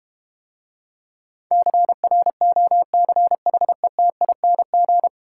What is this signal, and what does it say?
Morse code, heard as a keyed tone, about 30 wpm, 700 Hz. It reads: CROCHETING
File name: Easy Words - Speed-Racing with Hints - 25wpm